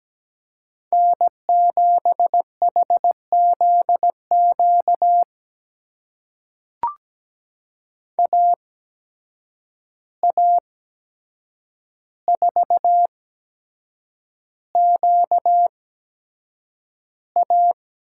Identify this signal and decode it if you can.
Morse code, heard as a keyed tone, about 17 wpm, spaced as for 7 wpm, 700 Hz. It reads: N7HZQ AA4QA
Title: USA Call Signs - 5 Characters - 8x Character Spacing - 17wpm